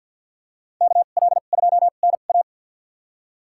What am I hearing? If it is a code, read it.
Morse code, heard as a keyed tone, about 52 wpm, spaced as for 36 wpm, 690 Hz. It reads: KL2NA
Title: USA Call Signs - 5 Characters - 2x Character Spacing - 50wpm